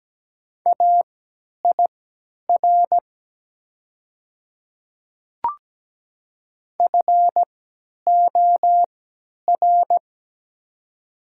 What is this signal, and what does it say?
Morse code, heard as a keyed tone, about 17 wpm, spaced as for 10 wpm, 700 Hz. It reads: AIR FOR
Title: 3x Character Spacing - Top 200 Words - 17wpm